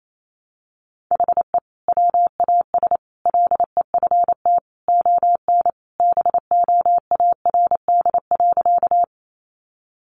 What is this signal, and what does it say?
Morse code, heard as a keyed tone, about 28 wpm, 700 Hz. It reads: HE WAS LEFT ON BOARD.
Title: Sentences from Top 700 Words 28wpm